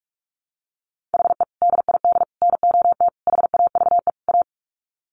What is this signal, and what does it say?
Morse code, heard as a keyed tone, about 44 wpm, 700 Hz. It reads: HE DID NOT HAVE A
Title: Sets of 5 Words Encore 45wpm